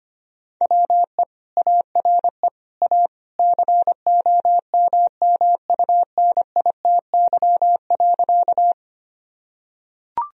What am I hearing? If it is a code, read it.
Morse code, heard as a keyed tone, about 25 wpm, 700 Hz. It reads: WE ARE A COMMUNITY.
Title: Sentences from Top 1000 Words - Review Part 4 - 25wpm